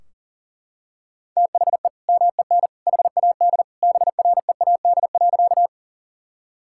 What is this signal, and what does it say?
Morse code, heard as a keyed tone, about 40 wpm, 700 Hz. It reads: THE MEN HAD BREAD.
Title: Sentences from Top 1600 Words 40wpm